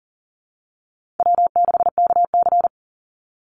Morse code, heard as a keyed tone, about 40 wpm, 700 Hz. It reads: W6KC
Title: Call Signs Repeated 3x - 40wpm